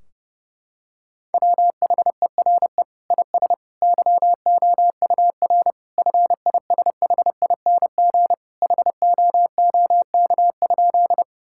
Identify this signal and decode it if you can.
Morse code, heard as a keyed tone, about 30 wpm, 700 Hz. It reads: WHERE IS YOUR FISHING HOOK?
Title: Sentences from Top 1600 Words 30wpm